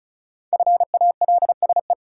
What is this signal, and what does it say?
Morse code, heard as a keyed tone, about 36 wpm, 700 Hz. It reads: FALSE
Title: Sentences from Top 1600 Words - New Word Drill - 35wpm